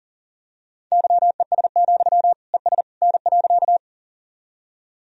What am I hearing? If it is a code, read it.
Morse code, heard as a keyed tone, about 37 wpm, 700 Hz. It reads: YES, ES N.
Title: Sentences from Top 500 Words and Abbreviations 40wpm